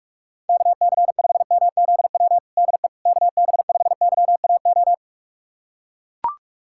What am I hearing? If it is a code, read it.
Morse code, heard as a keyed tone, about 41 wpm, 700 Hz. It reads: KC5MZW DE KB5YAO
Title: USA Callsign DE Callsign - 6 Characters - 45wpm